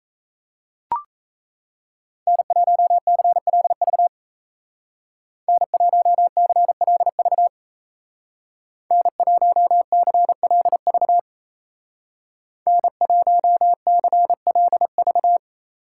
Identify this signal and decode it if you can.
Morse code, heard as a keyed tone, about 35 wpm, 700 Hz. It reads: N1CLV N1CLV N1CLV N1CLV
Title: Speed Racing - USA 1x3 Call Signs 28wpm